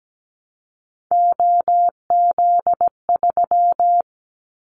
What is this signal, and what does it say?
Morse code, heard as a keyed tone, about 17 wpm, 700 Hz. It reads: OZ3
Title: Callsign Prefixes for Top 100 DXCC - 17wpm